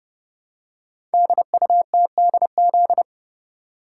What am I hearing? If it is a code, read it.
Morse code, heard as a keyed tone, about 30 wpm, 700 Hz. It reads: DUTDZ
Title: Code Groups - 5 Letters - 30wpm